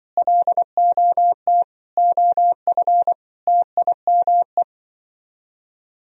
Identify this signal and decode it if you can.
Morse code, heard as a keyed tone, about 24 wpm, 700 Hz. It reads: LOT OF TIME
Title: Speed-Racing - Sets of 5 Words 20wpm